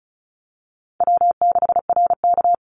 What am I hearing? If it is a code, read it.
Morse code, heard as a keyed tone, about 34 wpm, 700 Hz. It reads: W6RK
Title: Call Signs Repeated 2x - 35wpm